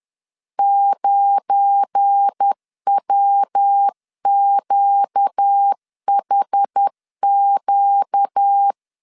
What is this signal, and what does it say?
Morse code, heard as a keyed tone, about 10 wpm, 800 Hz. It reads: 9WQHQ
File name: LEZIONI / 10 SIGNS / 10B INT 10wpm 2844